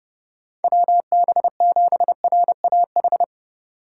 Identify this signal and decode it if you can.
Morse code, heard as a keyed tone, about 36 wpm, 700 Hz. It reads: WB7RAH
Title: Speed Racing - USA 2x3 Call Signs 30wpm